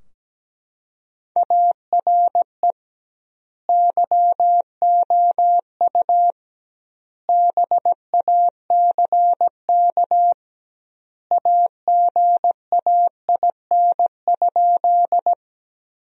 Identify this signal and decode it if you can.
Morse code, heard as a keyed tone, about 17 wpm, 700 Hz. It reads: ARE YOU BACK AGAIN?
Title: 2x Word-Spacing - Sentences from Top 200 Words - 17wpm